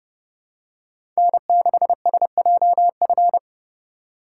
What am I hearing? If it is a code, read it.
Morse code, heard as a keyed tone, about 29 wpm, 700 Hz. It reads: N6SJF